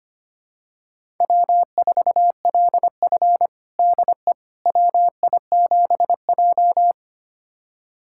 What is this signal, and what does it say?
Morse code, heard as a keyed tone, about 25 wpm, 700 Hz. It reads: W4LF DE WI7J
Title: USA Callsign DE Callsign - 4 Characters - 25wpm